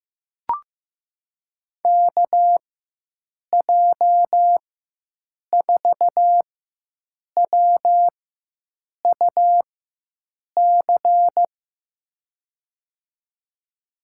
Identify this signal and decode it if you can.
Morse code, heard as a keyed tone, about 15 wpm, spaced as for 7 wpm, 700 Hz. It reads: KJ4WUC
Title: USA Call Signs - 4x Character Spacing - 15wpm